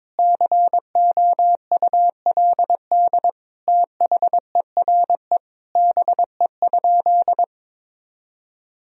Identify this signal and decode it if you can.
Morse code, heard as a keyed tone, about 22 wpm, 700 Hz. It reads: COULD THERE BE?
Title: Sentences from Top 1000 Words - Review Part 3 - 22wpm